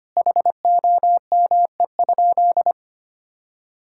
Morse code, heard as a keyed tone, about 25 wpm, 700 Hz. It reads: HOME?
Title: Sentences from Top 1600 Words 25wpm